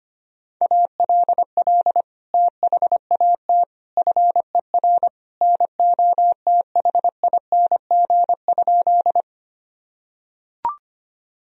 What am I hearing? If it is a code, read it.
Morse code, heard as a keyed tone, about 25 wpm, 700 Hz. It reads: ALL THAT FER NOTHING?
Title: Sentences from Top 500 Words and Abbreviations 25wpm